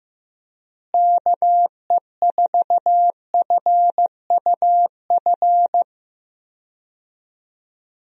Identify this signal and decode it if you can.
Morse code, heard as a keyed tone, about 16 wpm, 700 Hz. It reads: KE4FUF
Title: USA Call Signs - 6 Characters - Repeated 2x - 15wpm